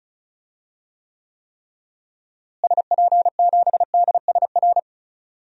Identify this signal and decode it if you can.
Morse code, heard as a keyed tone, about 36 wpm, 700 Hz. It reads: SP7DSR
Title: DXCC-100 Call Signs - 6 Characters - Repeated 3x - 35wpm